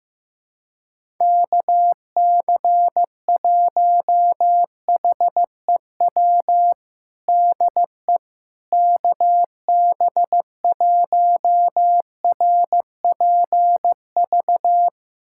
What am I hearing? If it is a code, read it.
Morse code, heard as a keyed tone, about 16 wpm, 700 Hz. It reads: KC1HEW DE KB1RPV
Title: USA Callsign DE Callsign - 6 Characters - 15wpm